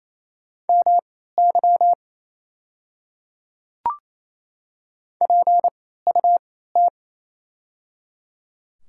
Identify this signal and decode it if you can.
Morse code, heard as a keyed tone, about 28 wpm, spaced as for 16 wpm, 700 Hz. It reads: MY PUT
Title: 3x Character Spacing - Top 300 Words - 28wpm